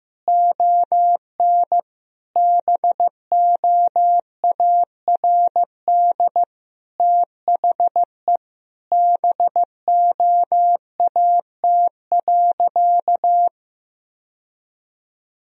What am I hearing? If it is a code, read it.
Morse code, heard as a keyed tone, about 15 wpm, 700 Hz. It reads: ON BOARD THE BOAT.
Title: Sentences from Top 1000 Words - Review Part 4 - 15wpm